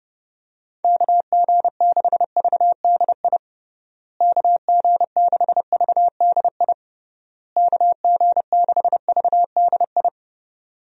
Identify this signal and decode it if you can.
Morse code, heard as a keyed tone, about 32 wpm, 700 Hz. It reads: KG6VDI KG6VDI KG6VDI
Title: USA Call Signs - 6 Characters - Repeated 3x - 30wpm